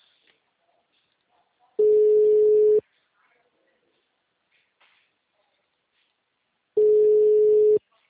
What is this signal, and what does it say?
Ringback tone (Europe, 425 Hz).
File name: out- -2006- - - .1296